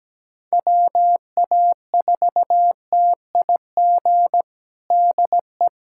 Morse code, heard as a keyed tone, about 17 wpm, 700 Hz. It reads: WA4TIG DE
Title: USA Callsign DE Callsign 17wpm